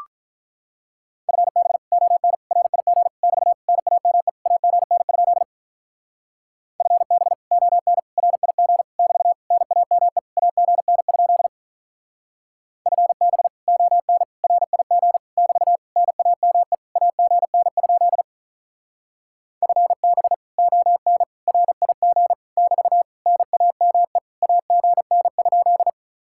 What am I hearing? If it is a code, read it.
Morse code, heard as a keyed tone, about 44 wpm, 700 Hz. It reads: FB ON RIG <BT> NAME AGN? FB ON RIG <BT> NAME AGN? FB ON RIG <BT> NAME AGN? FB ON RIG <BT> NAME AGN?